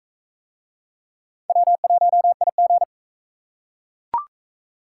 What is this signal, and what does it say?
Morse code, heard as a keyed tone, about 42 wpm, 700 Hz. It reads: W1IG